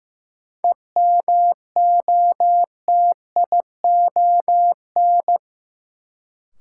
Morse code, heard as a keyed tone, about 15 wpm, 700 Hz. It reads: EMOTION
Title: Sentences from Top 1600 Words - New Word Drill - 15wpm